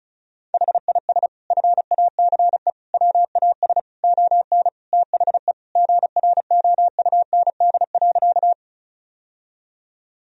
Morse code, heard as a keyed tone, about 36 wpm, 700 Hz. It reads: HIS FACE WAS ON THE GROUND.
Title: Sentences from Top 1000 Words - Review Part 2 - 35wpm